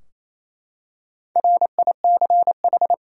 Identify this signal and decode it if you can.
Morse code, heard as a keyed tone, about 28 wpm, 700 Hz. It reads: RICH
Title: USA Common Names - Max 5 Letters - 28wpm